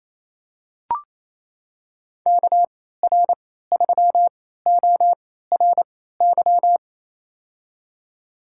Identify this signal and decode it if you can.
Morse code, heard as a keyed tone, about 28 wpm, spaced as for 16 wpm, 700 Hz. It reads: KR3ORY
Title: USA Call Signs - 6 Characters - 3x Character Spacing - 28wpm